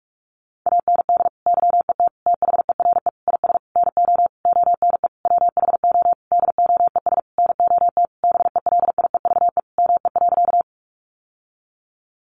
Morse code, heard as a keyed tone, about 44 wpm, 700 Hz. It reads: AND YET THERE IS NO ONE WHO DOES NOT BELIEVE ME.
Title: Sentences from Top 1000 Words 45wpm